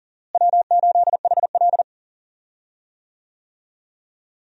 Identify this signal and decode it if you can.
Morse code, heard as a keyed tone, about 40 wpm, 700 Hz. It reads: W8HL
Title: USA Call Signs - 4 Characters - Repeated 2x - 40wpm